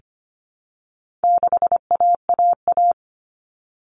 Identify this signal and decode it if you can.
Morse code, heard as a keyed tone, about 25 wpm, 700 Hz. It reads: 6AAA
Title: Common Cut Numbers - 4-Digit 25wpm